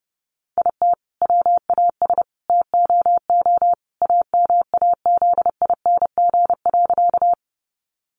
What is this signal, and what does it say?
Morse code, heard as a keyed tone, about 30 wpm, 700 Hz. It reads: IT WAS TOO AMAZING.
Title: Sentences from Top 1500 Words 30wpm